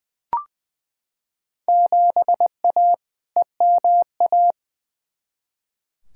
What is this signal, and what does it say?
Morse code, heard as a keyed tone, about 21 wpm, 700 Hz. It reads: 7A EMA
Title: Field Day Exchanges Encore - 20wpm